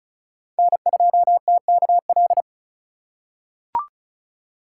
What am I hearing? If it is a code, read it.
Morse code, heard as a keyed tone, about 36 wpm, 700 Hz. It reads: N2TKL
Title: USA Call Signs - 4x Character Spacing - 35wpm